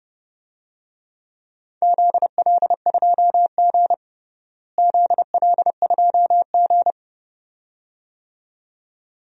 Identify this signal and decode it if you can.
Morse code, heard as a keyed tone, about 30 wpm, 700 Hz. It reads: ZL2G ZL2G
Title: DXCC-100 Call Signs - 4 Characters - Repeated 2x - 30wpm